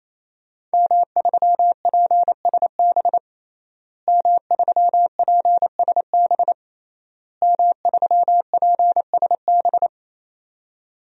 Morse code, heard as a keyed tone, about 28 wpm, 700 Hz. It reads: M3PSB M3PSB M3PSB
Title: DXCC-100 Call Signs - 5 Characters - Repeated 3x - 28wpm